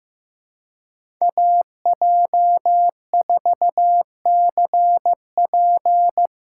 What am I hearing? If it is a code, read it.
Morse code, heard as a keyed tone, about 15 wpm, 700 Hz. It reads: AJ4CP
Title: USA Call Signs - 5 Characters - 3x Character Spacing - 15wpm